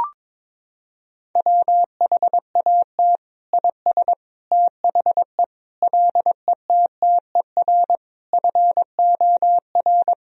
Morse code, heard as a keyed tone, about 22 wpm, 700 Hz. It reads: WHAT IS THE LETTER FOR
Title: Sentences from Top 1000 Words - Review Part 2 - 22wpm